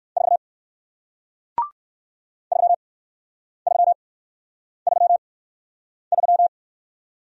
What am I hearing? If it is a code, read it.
Morse code, heard as a keyed tone, about 57 wpm, spaced as for 9 wpm, 700 Hz. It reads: 4 3333